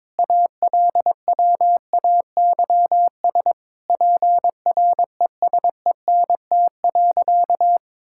Morse code, heard as a keyed tone, about 21 wpm, 700 Hz. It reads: ALWAYS PRESENT.
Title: Sentences from Top 1000 Words - Review Part 5 - 22wpm